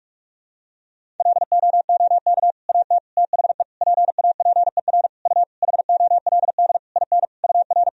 Morse code, heard as a keyed tone, about 41 wpm, 700 Hz. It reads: LOOK AT THE PAPER U HOLD IN UR
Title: Sentences from Top 400 Words and Abbreviations 45wpm